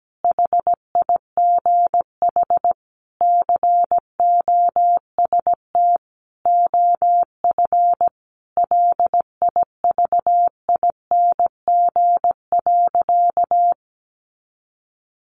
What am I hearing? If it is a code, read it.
Morse code, heard as a keyed tone, about 17 wpm, 700 Hz. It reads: HIGH COST OF LIVING.
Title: Sentences from Top 1400 Words 17wpm